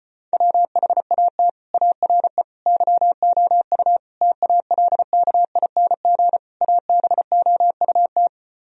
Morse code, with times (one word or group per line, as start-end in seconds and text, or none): 0.3-1.5 WHAT
1.7-2.4 ARE
2.7-4.0 YOU
4.2-6.4 TALKING
6.6-8.3 ABOUT